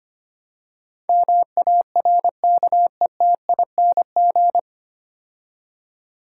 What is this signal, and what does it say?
Morse code, heard as a keyed tone, about 25 wpm, 700 Hz. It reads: MARKETING
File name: practice / 25wpm / 9 Letter Words - First 3 Letters Given - 25wpm